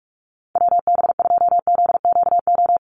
Morse code, heard as a keyed tone, about 44 wpm, 700 Hz. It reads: WB2ZQO
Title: USA Call Signs 45wpm